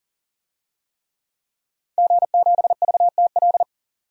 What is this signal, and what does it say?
Morse code, heard as a keyed tone, about 40 wpm, 700 Hz. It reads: G7VTL
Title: DXCC-100 Call Signs - 5 Characters - 40wpm